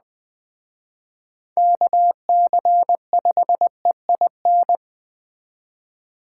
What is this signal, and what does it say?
Morse code, heard as a keyed tone, about 20 wpm, 700 Hz. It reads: KC5EIN
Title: Speed Racing - USA 2x3 Call Signs 17wpm